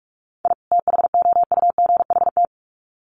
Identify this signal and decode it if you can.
Morse code, heard as a keyed tone, about 46 wpm, 700 Hz. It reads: I THOUGHT
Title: Sets of 2 Words 45wpm